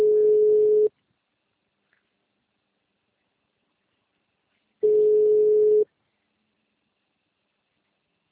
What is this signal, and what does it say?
Ringback tone (Europe, 425 Hz).